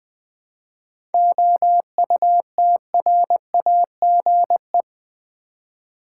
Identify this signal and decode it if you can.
Morse code, heard as a keyed tone, about 20 wpm, 700 Hz. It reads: OUTRAGE
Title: 7 Letter Words - First 3 Letters Given - 20wpm